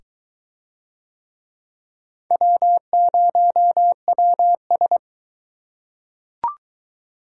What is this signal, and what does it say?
Morse code, heard as a keyed tone, about 20 wpm, 700 Hz. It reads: W0WS